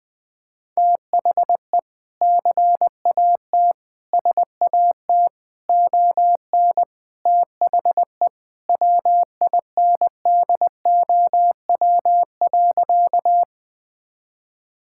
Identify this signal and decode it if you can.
Morse code, heard as a keyed tone, about 20 wpm, 700 Hz. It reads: THE CAT SAT ON THE WINDOW.
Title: Sentences with 2-Word Phrases - Spoken Beginning - 20wpm